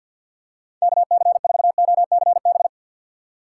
Morse code, heard as a keyed tone, about 48 wpm, 700 Hz. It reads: KC4OCB
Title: USA Call Signs - 6 Characters - 6x Character Spacing - 50wpm